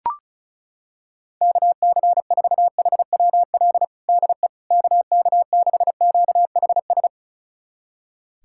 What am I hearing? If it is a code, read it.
Morse code, heard as a keyed tone, about 36 wpm, 700 Hz. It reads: KC4HWL DE KK6QHS
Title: USA Callsign DE Callsign 35wpm